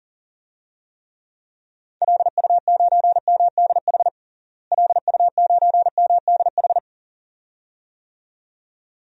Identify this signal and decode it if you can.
Morse code, heard as a keyed tone, about 40 wpm, 700 Hz. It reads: LU9MDH LU9MDH
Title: DXCC-100 Call Signs - 6 Characters - Repeated 2x - 40wpm